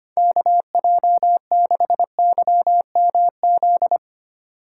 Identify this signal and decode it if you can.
Morse code, heard as a keyed tone, about 25 wpm, 700 Hz. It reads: KJ6YMZ
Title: USA Call Signs - 6 Characters - 8x Character Spacing - 25wpm